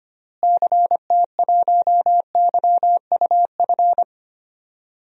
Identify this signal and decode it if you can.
Morse code, heard as a keyed tone, about 25 wpm, 700 Hz. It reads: CT1YUF
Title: DXCC-100 Call Signs - 6 Characters - Repeated 3x - 25wpm